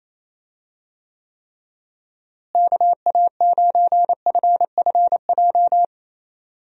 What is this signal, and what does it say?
Morse code, heard as a keyed tone, about 28 wpm, 700 Hz. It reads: KA9FFJ
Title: USA Call Signs - 6 Characters - Repeated 3x - 28wpm